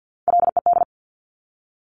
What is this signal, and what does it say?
Morse code, heard as a keyed tone, about 50 wpm, 700 Hz. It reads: LED
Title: Top Three Letter Combinations in Words 50wpm